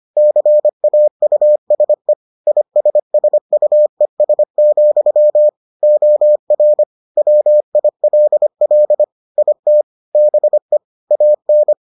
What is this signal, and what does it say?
Morse code, heard as a keyed tone, about 25 wpm, 600 Hz. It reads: CAUSE ISSUES, OR WILL IT BE AN